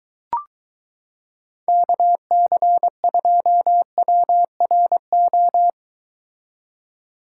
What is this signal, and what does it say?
Morse code, heard as a keyed tone, about 22 wpm, 700 Hz. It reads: KC2WRO